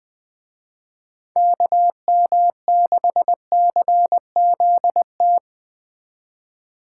Morse code, heard as a keyed tone, about 19 wpm, 700 Hz. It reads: KM6CZT
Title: Speed Racing - USA 2x3 Call Signs 17wpm